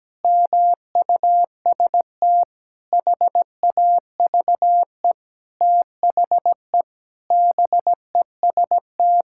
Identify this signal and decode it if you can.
Morse code, heard as a keyed tone, about 17 wpm, 700 Hz. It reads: MUST HAVE THE BEST
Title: Sentences from Top 1000 Words - Review Part 6 - 17wpm